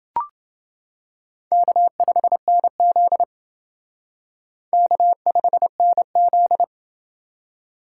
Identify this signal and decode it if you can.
Morse code, heard as a keyed tone, about 29 wpm, 700 Hz. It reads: K5NZ K5NZ